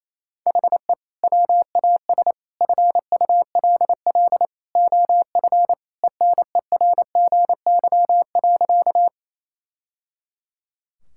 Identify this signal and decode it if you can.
Morse code, heard as a keyed tone, about 28 wpm, 700 Hz. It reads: HE WAS FULL OF ENERGY.
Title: Sentences from Top 1000 Words - Review Part 6 - 28wpm